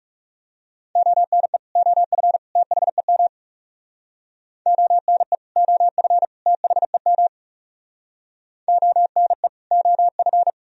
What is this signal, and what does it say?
Morse code, heard as a keyed tone, about 37 wpm, 700 Hz. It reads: ONE OF THEM ONE OF THEM ONE OF
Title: Speed-Racing - Sets of 3 Words 30wpm